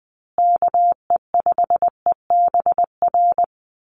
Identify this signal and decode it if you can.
Morse code, heard as a keyed tone, about 20 wpm, 700 Hz. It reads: KE5EBR
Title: USA Call Signs Repeated 2x 20wpm